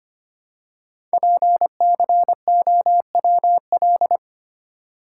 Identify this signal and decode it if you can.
Morse code, heard as a keyed tone, about 25 wpm, 700 Hz. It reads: PCOWL
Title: Code Groups - 5 Letters - 25wpm